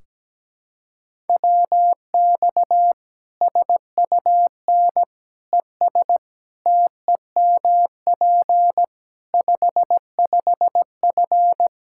Morse code, heard as a keyed tone, about 17 wpm, 700 Hz. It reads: WX SUN ES TEMP 55F
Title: Practice QSO - Exchange 2 - Info - 17wpm